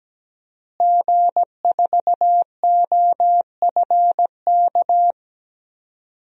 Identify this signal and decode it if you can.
Morse code, heard as a keyed tone, about 17 wpm, 700 Hz. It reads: G4OFK
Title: DXCC-100 Call Signs - 5 Characters - Repeated 3x - 17wpm